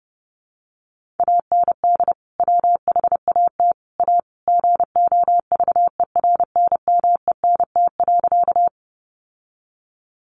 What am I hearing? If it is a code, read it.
Morse code, heard as a keyed tone, about 30 wpm, 700 Hz. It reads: AND WHAT A GOVERNMENT.
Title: Sentences from Top 600 Words 30wpm